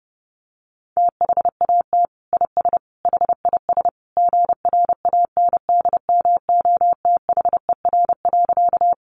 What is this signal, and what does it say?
Morse code, heard as a keyed tone, about 30 wpm, 700 Hz. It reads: THAT IS HIS GRANDMOTHER.
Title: Sentences from Top 1200 Words 30wpm